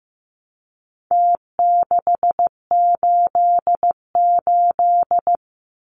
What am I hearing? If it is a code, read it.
Morse code, heard as a keyed tone, about 15 wpm, 700 Hz. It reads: T688
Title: Numbers - Contest Style - 4-Digit 15wpm